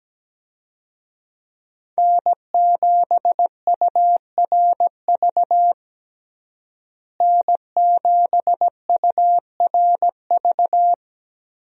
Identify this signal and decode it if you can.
Morse code, heard as a keyed tone, about 17 wpm, 700 Hz. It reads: N7URV N7URV
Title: USA Call Signs - 5 Characters - Repeated 3x - 17wpm